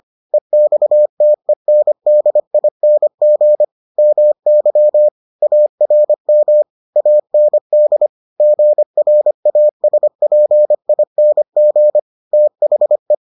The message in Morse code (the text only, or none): EXTENDING MY ARM AND GRASPING THE